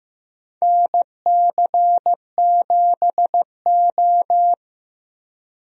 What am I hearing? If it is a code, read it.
Morse code, heard as a keyed tone, about 15 wpm, 700 Hz. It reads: NC7O
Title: USA Call Signs - 4 Characters - Repeated 3x - 15wpm